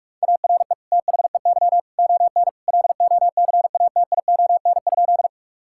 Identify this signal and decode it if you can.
Morse code, heard as a keyed tone, about 44 wpm, 700 Hz. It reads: ARE THEY ON LOCATION?
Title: Sentences from Top 1000 Words - Review Part 6 - 45wpm